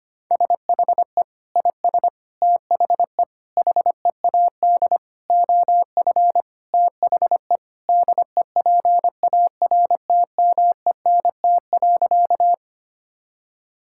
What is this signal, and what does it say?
Morse code, heard as a keyed tone, about 25 wpm, 700 Hz. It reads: SHE IS THE HEAD OF THE DEPARTMENT.
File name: practice / 25wpm / Sentences with 3-Word Phrases - 25wpm